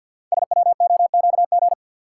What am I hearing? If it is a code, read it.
Morse code, heard as a keyed tone, about 48 wpm, 700 Hz. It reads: SWOQG
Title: Code Groups - 5 Letters - 50wpm